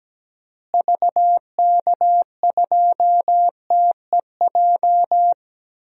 Morse code, heard as a keyed tone, about 17 wpm, 700 Hz. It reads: VK2TEJ
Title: DXCC-100 Call Signs - 6 Characters - Repeated 2x - 17wpm